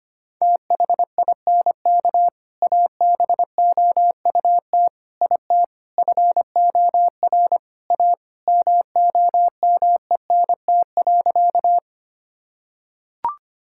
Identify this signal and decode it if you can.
Morse code, heard as a keyed tone, about 25 wpm, 700 Hz. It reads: THINK ABOUT IT FOR A MOMENT.
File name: practice / 25wpm / Sentences with 3-Word Phrases - 25wpm